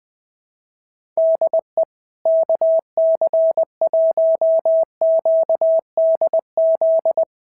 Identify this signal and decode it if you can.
Morse code, heard as a keyed tone, about 20 wpm, 650 Hz. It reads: DE KC1QDZ
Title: POTA QSO Practice - 20wpm